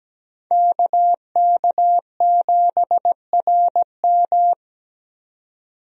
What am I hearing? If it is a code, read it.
Morse code, heard as a keyed tone, about 17 wpm, 700 Hz. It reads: KK7RM
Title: USA Call Signs - 5 Characters - 17wpm